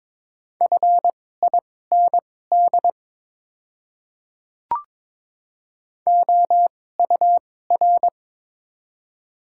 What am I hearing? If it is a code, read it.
Morse code, heard as a keyed tone, about 21 wpm, spaced as for 16 wpm, 700 Hz. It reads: FIND OUR
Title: 2x Character Spacing - Top 300 Words - 22wpm